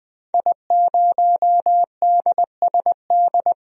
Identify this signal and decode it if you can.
Morse code, heard as a keyed tone, about 20 wpm, 700 Hz. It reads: I0DSD
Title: DXCC-100 Call Signs - 5 Characters - 20wpm